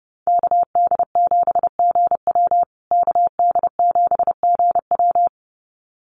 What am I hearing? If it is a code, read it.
Morse code, heard as a keyed tone, about 28 wpm, 700 Hz. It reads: KD7GW KD7GW
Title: USA Call Signs Repeated 2x 30wpm